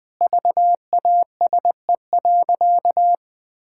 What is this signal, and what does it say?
Morse code, heard as a keyed tone, about 20 wpm, 700 Hz. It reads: VASE.
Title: Pangram Challenge v2 - 20wpm